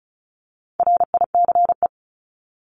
Morse code, heard as a keyed tone, about 34 wpm, 700 Hz. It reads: RICE
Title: Top 1501-1750 Words 35wpm